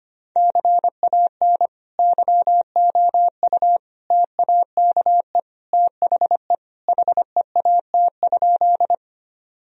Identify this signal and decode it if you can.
Morse code, heard as a keyed tone, about 25 wpm, 700 Hz. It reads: CAN YOU TAKE THE HEAT?
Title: Sentences from Top 1000 Words - Review Part 3 - 25wpm